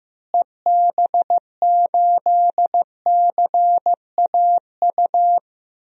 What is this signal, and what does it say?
Morse code, heard as a keyed tone, about 15 wpm, 700 Hz. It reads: EB8CAU
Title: DXCC-100 Call Signs - 6 Characters - Repeated 3x - 15wpm